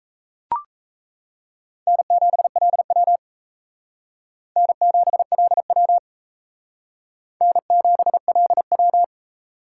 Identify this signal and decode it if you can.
Morse code, heard as a keyed tone, about 35 wpm, 700 Hz. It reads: N7LW N7LW N7LW
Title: Speed Racing - USA 1x2 Call Signs 28wpm